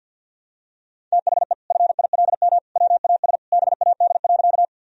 Morse code, heard as a keyed tone, about 52 wpm, 700 Hz. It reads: THE FILM WAS BAD.
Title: Sentences from Top 1000 Words - Review Part 3 - 50wpm